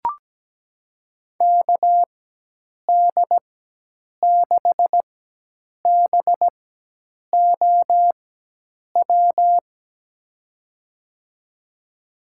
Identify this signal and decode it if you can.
Morse code, heard as a keyed tone, about 17 wpm, spaced as for 8 wpm, 700 Hz. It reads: KD6BOW
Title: USA Call Signs - 6 Characters - 4x Character Spacing - 17wpm